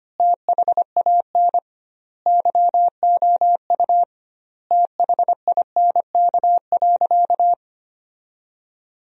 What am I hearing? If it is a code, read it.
Morse code, heard as a keyed tone, about 25 wpm, 700 Hz. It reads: THAN YOU THINK.
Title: 2x Word-Spacing - Sentences from Top 400 Words - 25wpm